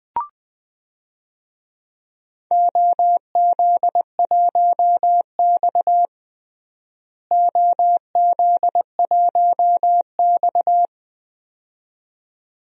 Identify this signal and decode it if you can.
Morse code, heard as a keyed tone, about 21 wpm, 700 Hz. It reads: OZ1X OZ1X
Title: DXCC-100 Call Signs - 4 Characters - Repeated 2x - 20wpm